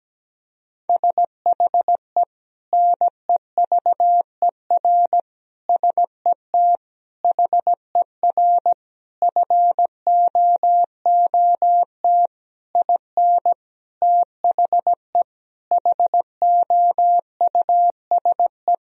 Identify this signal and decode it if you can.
Morse code, heard as a keyed tone, about 17 wpm, 700 Hz. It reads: SHE NEVER SET HER FOOT IN THE HOUSE
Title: Sentences from Top 1000 Words - Review Part 3 - 17wpm